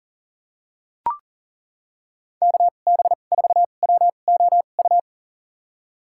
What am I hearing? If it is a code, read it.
Morse code, heard as a keyed tone, about 40 wpm, spaced as for 29 wpm, 700 Hz. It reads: KB4WOU
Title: USA Call Signs - 6 Characters - 2x Character Spacing - 40wpm